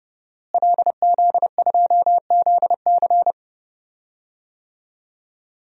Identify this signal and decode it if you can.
Morse code, heard as a keyed tone, about 31 wpm, 700 Hz. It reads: LZ2ZC
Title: DXCC-100 Call Signs - 5 Characters - Repeated 2x - 30wpm